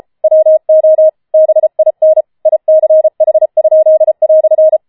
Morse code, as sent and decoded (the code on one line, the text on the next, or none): .-- --- / -... .. -. / .. -.-. .... ..--.. .-..-.
WO BIN ICH?"